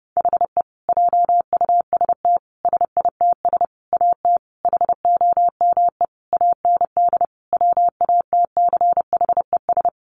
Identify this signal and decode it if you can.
Morse code, heard as a keyed tone, about 30 wpm, 700 Hz. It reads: HE JUST SITS AT HOME AND WATCHES
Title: Additional Easy Phrases 30wpm